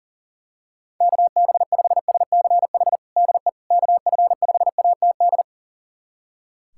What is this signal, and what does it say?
Morse code, heard as a keyed tone, about 38 wpm, 700 Hz. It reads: KB5SCH DE KF5ATD
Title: USA Callsign DE Callsign - 6 Characters - 40wpm